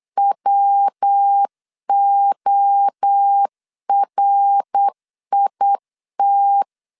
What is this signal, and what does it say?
Morse code, heard as a keyed tone, about 8 wpm, 800 Hz. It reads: WORIT
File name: CW / DET / 1 3 1500C S B 8wpm 4141